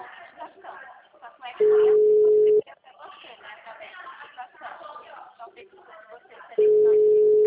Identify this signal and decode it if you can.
Ringback tone (Europe, 425 Hz), with speech.